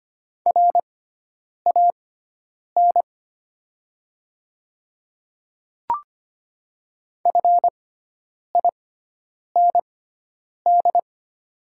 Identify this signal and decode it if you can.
Morse code, heard as a keyed tone, about 25 wpm, spaced as for 9 wpm, 690 Hz. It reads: RAN FIND
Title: 6x Character Spacing - Top 500 Words - 25wpm